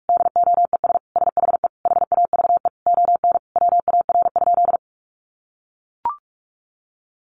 Morse code, heard as a keyed tone, about 44 wpm, 700 Hz. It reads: DOES SHE HAVE ON WAR?
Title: Sentences from Top 1200 Words 45wpm